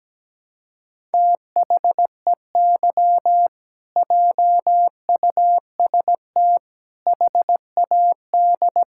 Morse code, heard as a keyed tone, about 17 wpm, 700 Hz. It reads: THEY JUST HAD
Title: Sentences from Top 1000 Words - Review Part 6 - 17wpm